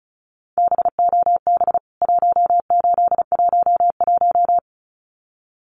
Morse code, heard as a keyed tone, about 34 wpm, 700 Hz. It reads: BOB 1811
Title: CWT Exchange 35wpm